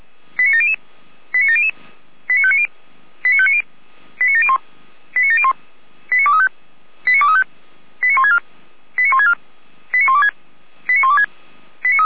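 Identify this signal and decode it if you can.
Five-tone selective call (ZVEI), heard as quick runs of five tones, tones of about 70 ms. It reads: 89700 89700 89590 89590 89701 89701 89235 89235 89164 89164 89126 89126
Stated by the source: ZVEI five-tone selective call 89094 received on SDR.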